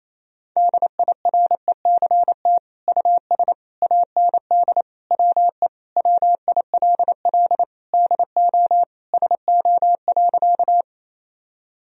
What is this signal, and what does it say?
Morse code, heard as a keyed tone, about 27 wpm, 700 Hz. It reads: DIRECT US AND WE WILL DO SO.
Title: Sentences from Top 1000 Words - Review Part 2 - 28wpm